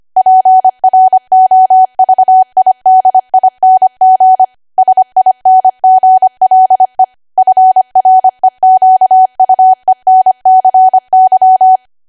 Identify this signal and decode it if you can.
Morse code, heard as a keyed tone, about 25 wpm, 750 Hz. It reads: PROVIDING SINGLE FREQUENCY
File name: Archive / 25 WPM / 25WPM